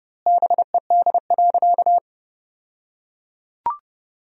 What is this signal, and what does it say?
Morse code, heard as a keyed tone, about 32 wpm, 700 Hz. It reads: BED.
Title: 2x Word-Spacing - Sentences from Top 500 Words - 30wpm